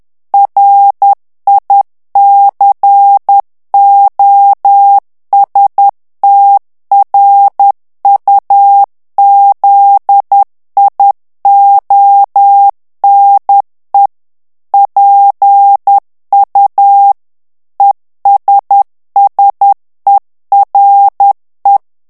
Morse code, heard as a keyed tone, about 10 wpm, 800 Hz. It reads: RICOSTRUZIONE PU ESSERE
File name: morse mp3 / ITALIEN texte clair 1 vitesse 10 mots minute